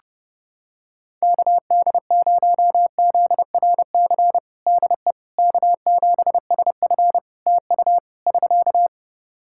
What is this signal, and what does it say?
Morse code, heard as a keyed tone, about 32 wpm, 700 Hz. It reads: KD0ZRC DE K7SF TU <SK>
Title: Practice QSO - Exchange 4 - Ending - 30wpm